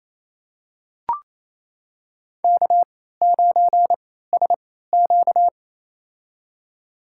Morse code, heard as a keyed tone, about 28 wpm, spaced as for 16 wpm, 700 Hz. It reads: K9SQ
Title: USA Call Signs - 4 Characters - 3x Character Spacing - 28wpm